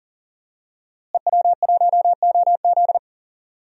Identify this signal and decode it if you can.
Morse code, heard as a keyed tone, about 37 wpm, 700 Hz. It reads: EW1OZ
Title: DXCC-100 Call Signs - 5 Characters - Repeated 3x - 40wpm